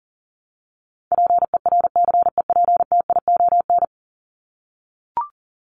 Morse code, heard as a keyed tone, about 40 wpm, 700 Hz. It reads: PERCEPTION
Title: Top 1751-2000 Words 40wpm